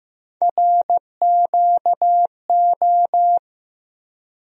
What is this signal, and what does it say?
Morse code, heard as a keyed tone, about 15 wpm, 700 Hz. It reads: RQO
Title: USA Call Sign Postfixes - 3 Letters 15wpm